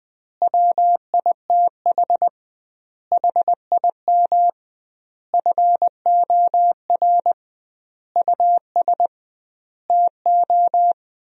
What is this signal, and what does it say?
Morse code, heard as a keyed tone, about 20 wpm, 700 Hz. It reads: WITH HIM FOR US TO
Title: 2x Word-Spacing - Sentences from Top 500 Words - 20wpm